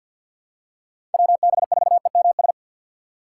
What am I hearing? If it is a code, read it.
Morse code, heard as a keyed tone, about 51 wpm, 700 Hz. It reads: WB4EMS
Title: USA Call Signs - 3x Character Spacing - 50wpm